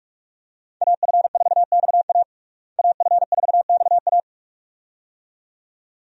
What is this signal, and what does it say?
Morse code, heard as a keyed tone, about 47 wpm, 700 Hz. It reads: AF4XA AF4XA
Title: USA Call Signs - 5 Characters - Repeated 2x - 45wpm